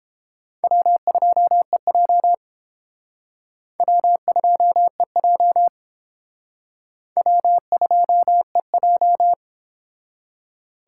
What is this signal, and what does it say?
Morse code, heard as a keyed tone, about 30 wpm, 700 Hz. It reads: W2EJ W2EJ W2EJ